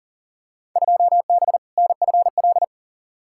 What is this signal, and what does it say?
Morse code, heard as a keyed tone, about 40 wpm, 700 Hz. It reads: 2B NFL
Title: Field Day Exchanges Encore - 40wpm